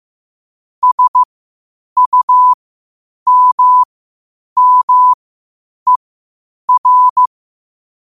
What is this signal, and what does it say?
Morse code, heard as a keyed tone, about 15 wpm, spaced as for 8 wpm, 1000 Hz. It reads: SUMMER
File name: qotdfiles / 8wpm